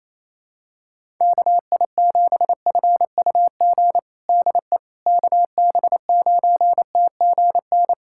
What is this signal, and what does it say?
Morse code, heard as a keyed tone, about 27 wpm, 700 Hz. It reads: KI7FUG DE KB9TGN
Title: USA Callsign DE Callsign - 6 Characters - 28wpm